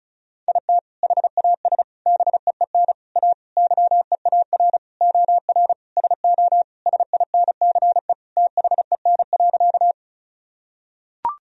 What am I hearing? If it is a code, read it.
Morse code, heard as a keyed tone, about 36 wpm, 700 Hz. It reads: IT HAS BEEN A YEAR OR SO SINCE THEN.
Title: Sentences from Top 1000 Words - Review Part 1 - 35wpm